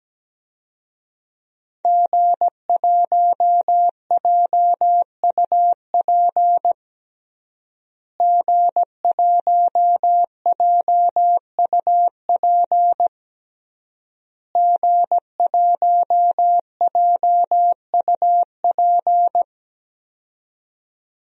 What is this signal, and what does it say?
Morse code, heard as a keyed tone, about 17 wpm, 700 Hz. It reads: G1JUP G1JUP G1JUP
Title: DXCC-100 Call Signs - 5 Characters - Repeated 3x - 17wpm